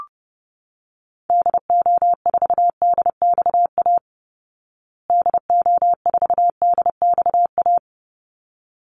Morse code, heard as a keyed tone, about 30 wpm, 700 Hz. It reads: DO4DXA DO4DXA
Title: Call Signs Repeated 3x - 30wpm